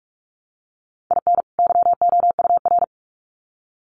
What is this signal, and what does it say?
Morse code, heard as a keyed tone, about 47 wpm, 700 Hz. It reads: IN YOUR
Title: Sets of 2 Words 45wpm